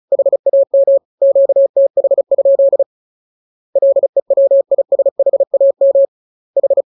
Morse code, heard as a keyed tone, about 34 wpm, 550 Hz. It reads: HAM QTH? LEWISHAM H
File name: jokes / joke 15 35 0000